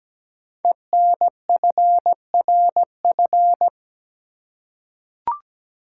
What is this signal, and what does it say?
Morse code, heard as a keyed tone, about 17 wpm, 700 Hz. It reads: ENFRF
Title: Code Groups - 5 Letters - 17wpm